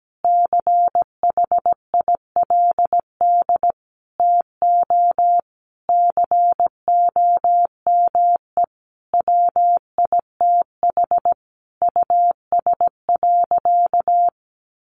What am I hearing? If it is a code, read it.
Morse code, heard as a keyed tone, about 17 wpm, 700 Hz. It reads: CHILD TO COME WITH US.
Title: Sentences from Top 1300 Words 17wpm